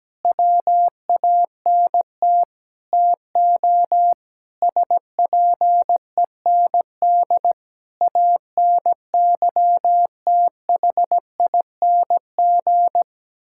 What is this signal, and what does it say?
Morse code, heard as a keyed tone, about 17 wpm, 700 Hz. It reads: WANT TO SPEND ANYTHING
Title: Sentences from Top 1000 Words - Review Part 5 - 17wpm